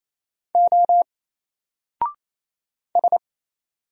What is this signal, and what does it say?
Morse code, heard as a keyed tone, about 27 wpm, 700 Hz. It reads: O S
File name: Single-Letter-Number-28wpm